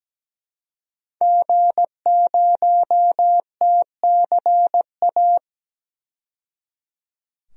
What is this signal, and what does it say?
Morse code, heard as a keyed tone, about 17 wpm, 700 Hz. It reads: G0TCA
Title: DXCC-100 Call Signs - 5 Characters - 17wpm